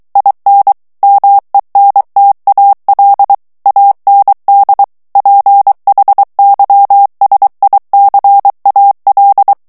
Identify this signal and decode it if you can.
Morse code, heard as a keyed tone, about 23 wpm, 800 Hz. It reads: IN MENTAL AND PHYSICAL
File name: morse mp3 / ANGLAIS texte clair 1 vitesse 22 mots minute